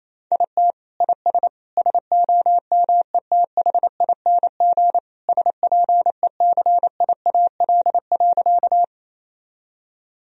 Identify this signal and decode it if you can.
Morse code, heard as a keyed tone, about 28 wpm, 700 Hz. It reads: IT IS SOMETHING SPECIAL.
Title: Sentences from Top 1000 Words - Review Part 3 - 28wpm